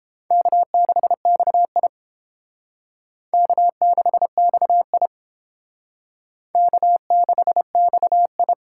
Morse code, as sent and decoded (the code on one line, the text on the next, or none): -.- -.... -..- .. / -.- -.... -..- .. / -.- -.... -..- ..
K6XI K6XI K6XI